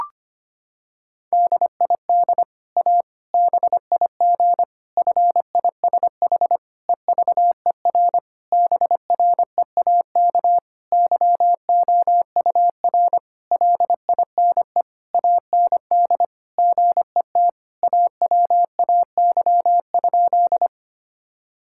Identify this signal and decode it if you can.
Morse code, heard as a keyed tone, about 25 wpm, 700 Hz. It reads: DID A BIG FISH EVER BREAK YOUR LINE AND GET AWAY?
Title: Sentences from Top 1000 Words - Review Part 6 - 25wpm